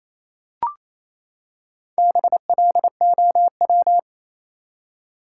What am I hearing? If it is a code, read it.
Morse code, heard as a keyed tone, about 28 wpm, 700 Hz. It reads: BLOW
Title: Four Letter Words - 28wpm